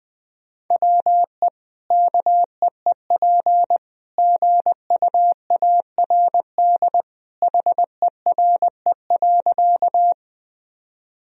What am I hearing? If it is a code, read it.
Morse code, heard as a keyed tone, about 21 wpm, 700 Hz. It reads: WE KEEP GUARD HERE.
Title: Sentences from Top 1000 Words - Review Part 4 - 20wpm